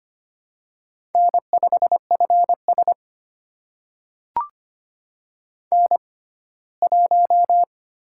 Morse code, heard as a keyed tone, about 25 wpm, spaced as for 17 wpm, 700 Hz. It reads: N5FS N1
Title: USA Call Signs - 4 Characters - 6x Character Spacing - 25wpm